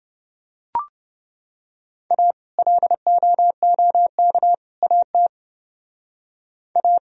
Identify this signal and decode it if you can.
Morse code, heard as a keyed tone, about 28 wpm, 700 Hz. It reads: A LOOK AT A